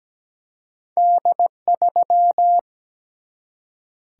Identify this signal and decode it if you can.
Morse code, heard as a keyed tone, about 17 wpm, 700 Hz. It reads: D3
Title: Single-Letter Single-Number 17wpm